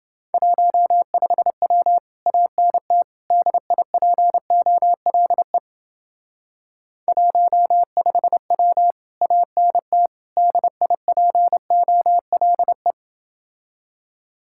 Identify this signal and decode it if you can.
Morse code, heard as a keyed tone, about 30 wpm, 700 Hz. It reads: 15W ANT DIPOLE 15W ANT DIPOLE